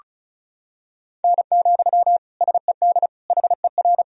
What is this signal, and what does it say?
Morse code, heard as a keyed tone, about 36 wpm, 700 Hz. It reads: N, SED HER
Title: Sentences from Top 300 Words and Abbreviations 35wpm